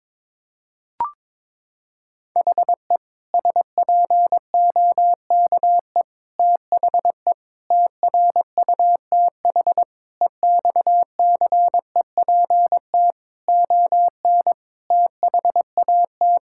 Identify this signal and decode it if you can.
Morse code, heard as a keyed tone, about 22 wpm, 700 Hz. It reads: HE SPOKE THE TRUTH EXCEPT ON THAT
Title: Sentences from Top 1600 Words 22wpm